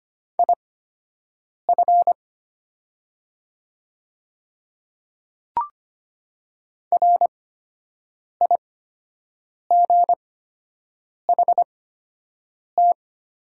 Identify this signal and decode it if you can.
Morse code, heard as a keyed tone, about 25 wpm, spaced as for 7 wpm, 700 Hz. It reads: IF RIGHT
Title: 8x Character Spacing - Top 200 Words - 25wpm